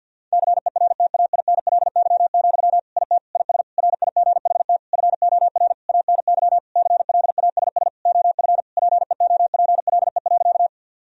Morse code, heard as a keyed tone, about 48 wpm, 700 Hz. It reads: CERTAINLY, IT IS RIGHT FOR ANY CLASS OF PEOPLE.
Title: Sentences from Top 1000 Words - Review Part 2 - 50wpm